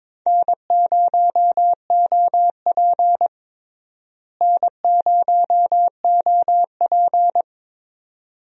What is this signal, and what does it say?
Morse code, heard as a keyed tone, about 22 wpm, 700 Hz. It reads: N0OP N0OP
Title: USA Call Signs - 4 Characters - Repeated 2x - 22wpm